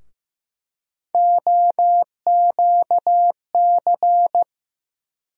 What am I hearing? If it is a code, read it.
Morse code, heard as a keyed tone, about 15 wpm, 700 Hz. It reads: OQC
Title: USA Call Sign Postfixes - 3 Letters 15wpm